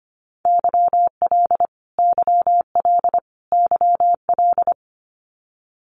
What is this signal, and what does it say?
Morse code, heard as a keyed tone, about 25 wpm, 700 Hz. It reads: YL YL YL
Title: Abbreviations Repeated 3x - 25wpm